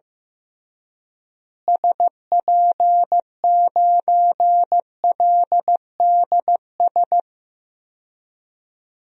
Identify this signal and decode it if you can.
Morse code, heard as a keyed tone, about 15 wpm, 700 Hz. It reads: SP9LDS
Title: DXCC-100 Call Signs - 6 Characters - Repeated 2x - 15wpm